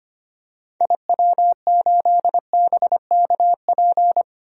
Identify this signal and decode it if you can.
Morse code, heard as a keyed tone, about 25 wpm, 700 Hz. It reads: IW8BKP
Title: DXCC-100 Call Signs - 6 Characters - 25wpm